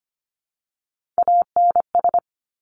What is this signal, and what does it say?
Morse code, heard as a keyed tone, about 25 wpm, 700 Hz. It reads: ANS
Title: Top Three Letter Combinations in Words 25wpm